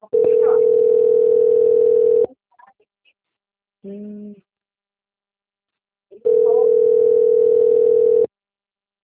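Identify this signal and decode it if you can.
Ringback tone (North America), 440 + 480 Hz.
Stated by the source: Outbound telephone call recorded on the Valinhos - SP phone system.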